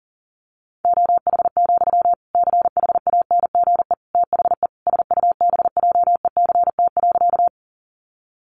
Morse code, heard as a keyed tone, about 40 wpm, 700 Hz. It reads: OH, CHANGE THE SUBJECT.
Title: Sentences from Top 800 Words 40wpm